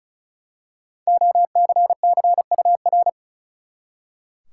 Morse code, heard as a keyed tone, about 36 wpm, 700 Hz. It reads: OCCUR
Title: Five Letter Words - 35wpm